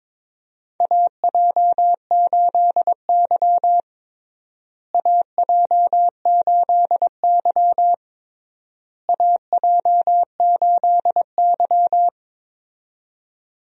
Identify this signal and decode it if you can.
Morse code, heard as a keyed tone, about 22 wpm, 700 Hz. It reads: AJ8Y AJ8Y AJ8Y
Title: USA Call Signs - 4 Characters - Repeated 3x - 22wpm